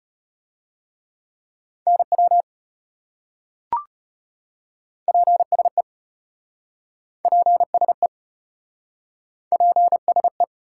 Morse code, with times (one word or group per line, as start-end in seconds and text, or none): 1.9-2.4 NW
5.1-5.8 PSE
7.2-8.1 PSE
9.5-10.4 PSE